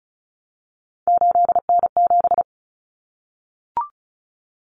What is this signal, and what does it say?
Morse code, heard as a keyed tone, about 34 wpm, 700 Hz. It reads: 8N7
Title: Common Cut Numbers - 3-Digit 35wpm